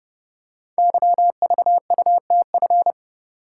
Morse code, heard as a keyed tone, about 30 wpm, 700 Hz. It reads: YVUTF
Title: Code Groups - 5 Letter:Numbers - 30wpm